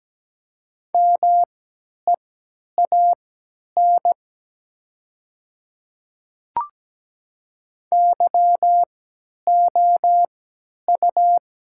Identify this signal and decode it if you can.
Morse code, heard as a keyed tone, about 17 wpm, spaced as for 10 wpm, 700 Hz. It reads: MEAN YOU
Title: 3x Character Spacing - Top 200 Words - 17wpm